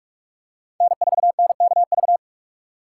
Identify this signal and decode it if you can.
Morse code, heard as a keyed tone, about 47 wpm, 700 Hz. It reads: N4NKV
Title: USA Call Signs - 5 Characters - 3x Character Spacing - 45wpm